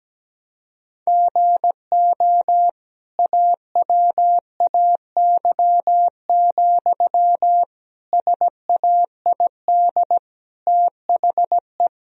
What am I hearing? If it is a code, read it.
Morse code, heard as a keyed tone, about 17 wpm, 700 Hz. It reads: GO AWAY, SAID THE
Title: Sentences from Top 1000 Words - Review Part 2 - 17wpm